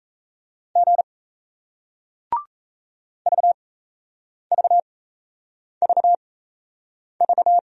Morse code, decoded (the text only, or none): G VVVV